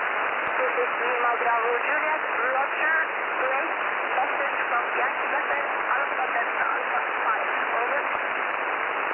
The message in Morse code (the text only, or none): none